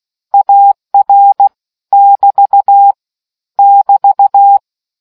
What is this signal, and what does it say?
Morse code, heard as a keyed tone, about 16 wpm, 800 Hz. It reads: AR <BT> <BT>